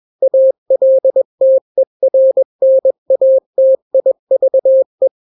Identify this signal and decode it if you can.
Morse code, heard as a keyed tone, about 21 wpm, 530 Hz. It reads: ALTERNATIVE
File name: daily / 20wpm